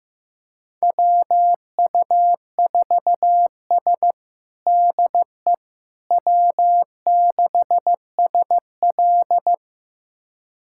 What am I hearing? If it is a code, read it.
Morse code, heard as a keyed tone, about 15 wpm, 700 Hz. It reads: WU4S DE W6SL
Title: USA Callsign DE Callsign - 4 Characters - 15wpm